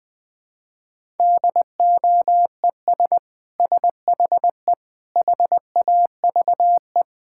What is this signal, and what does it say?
Morse code, heard as a keyed tone, about 21 wpm, 700 Hz. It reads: DOES SHE HAVE
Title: Sentences from Top 1600 Words 20wpm